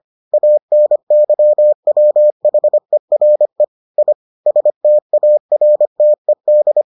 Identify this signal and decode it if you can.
Morse code, heard as a keyed tone, about 25 wpm, 600 Hz. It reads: ANYWHERE I STARTED